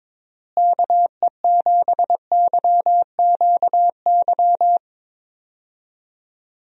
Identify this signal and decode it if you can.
Morse code, heard as a keyed tone, about 22 wpm, 700 Hz. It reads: KE7YQY
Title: USA Call Signs - 6 Characters - Repeated 3x - 22wpm